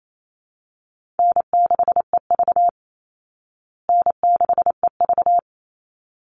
Morse code, heard as a keyed tone, about 28 wpm, 700 Hz. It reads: N6EV N6EV
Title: Call Signs Repeated 3x - 28wpm